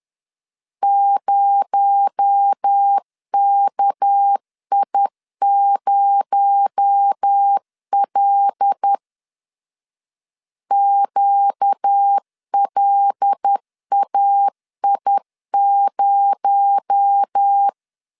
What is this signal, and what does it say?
Morse code, heard as a keyed tone, about 10 wpm, 800 Hz. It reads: 0KI0L QLAI0
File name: LEZIONI / 6 PKX0 / 6 PKX0 10wpm 2707